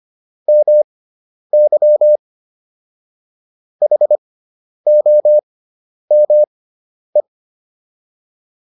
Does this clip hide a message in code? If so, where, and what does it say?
Yes: Morse code from 0.5 to 7.2 s, heard as a keyed tone, about 25 wpm, spaced as for 10 wpm, 600 Hz. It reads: MY HOME